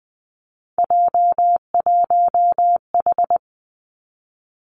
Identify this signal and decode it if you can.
Morse code, heard as a keyed tone, about 20 wpm, 700 Hz. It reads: J1H
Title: Three Challenging Letter-Number Combinations 20wpm